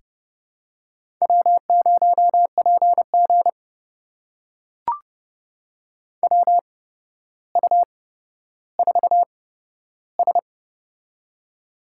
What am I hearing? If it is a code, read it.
Morse code, heard as a keyed tone, about 30 wpm, spaced as for 13 wpm, 700 Hz. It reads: W0PG WU4S
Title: USA Call Signs - 8x Character Spacing - 30wpm